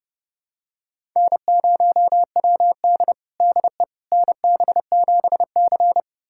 Morse code, heard as a keyed tone, about 32 wpm, 700 Hz. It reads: N0WD DE NB7C
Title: USA Callsign DE Callsign - 4 Characters - 30wpm